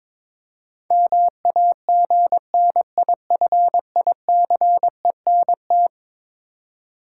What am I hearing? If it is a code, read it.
Morse code, heard as a keyed tone, about 22 wpm, 700 Hz. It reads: MAGNIFICENT
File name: practice / 22wpm / Eleven Letter Words - 22wpm